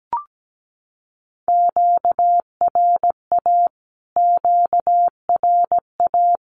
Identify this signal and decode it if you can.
Morse code, heard as a keyed tone, about 17 wpm, 700 Hz. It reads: QRA QRA
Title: Q-Codes Repeated 3x 17wpm